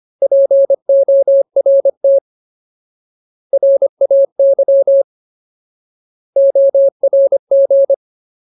Morse code, heard as a keyed tone, about 25 wpm, 550 Hz. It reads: PORT RAY ORG
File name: Ultimate Word List / 25WPM Ultimate